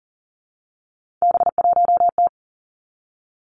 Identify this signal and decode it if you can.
Morse code, heard as a keyed tone, about 40 wpm, 700 Hz. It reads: B1T
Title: Three Challenging Letter-Number Combinations 40wpm